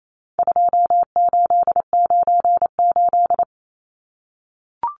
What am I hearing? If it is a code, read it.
Morse code, heard as a keyed tone, about 28 wpm, 700 Hz. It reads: 2898
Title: Numbers - 4-Digit 28wpm